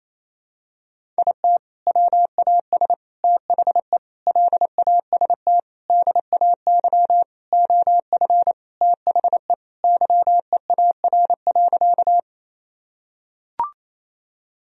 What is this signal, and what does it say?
Morse code, heard as a keyed tone, about 28 wpm, 700 Hz. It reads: IT WAS THE LAST DAY OF THE YEAR.
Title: Sentences from Top 1000 Words - Review Part 1 - 28wpm